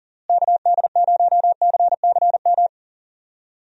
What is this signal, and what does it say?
Morse code, heard as a keyed tone, about 37 wpm, 700 Hz. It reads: KD0CCM
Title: USA Call Signs - 6 Characters - Repeated 2x - 40wpm